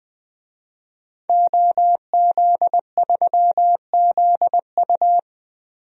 Morse code, heard as a keyed tone, about 21 wpm, 700 Hz. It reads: OZ3ZU
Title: DXCC-100 Call Signs - 5 Characters - 20wpm